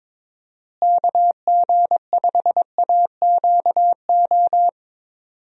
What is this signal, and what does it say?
Morse code, heard as a keyed tone, about 21 wpm, 700 Hz. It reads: KG5AQO
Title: USA Call Signs - 6 Characters - Repeated 3x - 22wpm